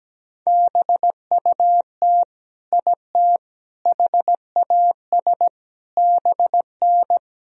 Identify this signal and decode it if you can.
Morse code, heard as a keyed tone, about 17 wpm, 700 Hz. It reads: BUT IT HAS BN
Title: Sentences from Top 100 Words and Abbreviations 17wpm